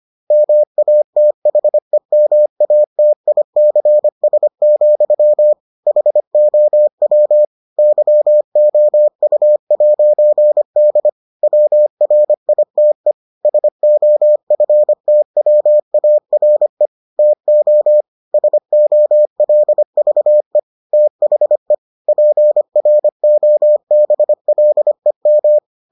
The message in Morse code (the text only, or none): MATHEMATICS, HOW YOU'D WRITE SOFTWARE TO SOLVE THE PROBLEM